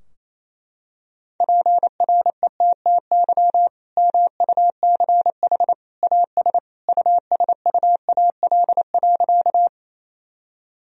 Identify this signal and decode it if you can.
Morse code, heard as a keyed tone, about 28 wpm, 700 Hz. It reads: PRETTY MUCH AS USUAL.
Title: Sentences from Top 1000 Words - Review Part 2 - 28wpm